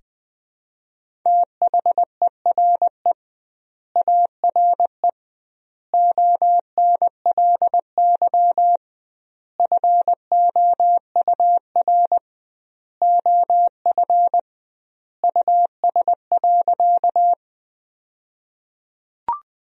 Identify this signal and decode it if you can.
Morse code, heard as a keyed tone, about 20 wpm, 700 Hz. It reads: THERE ARE ONLY FOUR OF US.
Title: 2x Word-Spacing - Sentences from Top 300 Words - 20wpm